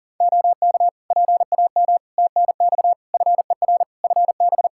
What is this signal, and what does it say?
Morse code, heard as a keyed tone, about 40 wpm, 700 Hz. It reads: OK PAM TNX FER FB
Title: K1USN SST QSO Practice 40wpm